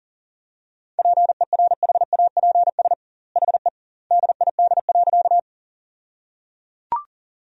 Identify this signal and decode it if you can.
Morse code, heard as a keyed tone, about 37 wpm, 700 Hz. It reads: PERHAPS HE DID.
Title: 2x Word-Spacing - Sentences from Top 500 Words - 40wpm